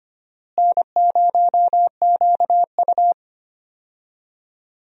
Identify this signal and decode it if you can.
Morse code, heard as a keyed tone, about 25 wpm, 700 Hz. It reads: N0QU
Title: USA Call Signs - 4 Characters - Repeated 2x - 25wpm